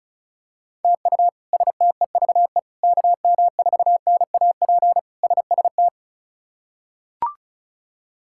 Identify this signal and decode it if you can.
Morse code, heard as a keyed tone, about 36 wpm, 700 Hz. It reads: TU STEVE KM4NAP SST
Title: K1USN SST QSO Practice - 35wpm